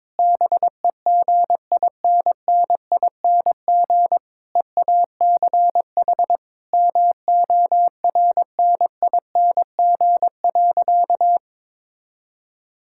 Morse code, heard as a keyed tone, about 22 wpm, 700 Hz. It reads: BEGINNING EACH MORNING.
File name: Difficult Sentences with Hints - 22wpm